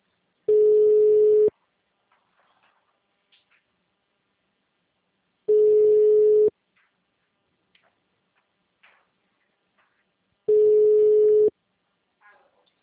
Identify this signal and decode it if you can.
Ringback tone (Europe, 425 Hz).
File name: out- -2009- - - .9137